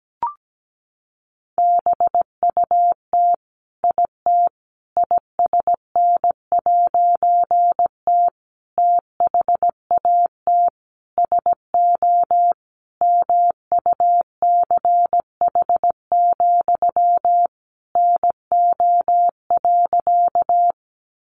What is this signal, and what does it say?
Morse code, heard as a keyed tone, about 17 wpm, 700 Hz. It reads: BUT IT ISN'T THAT SO MUCH, NO.
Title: A Sound Of Thunder 17wpm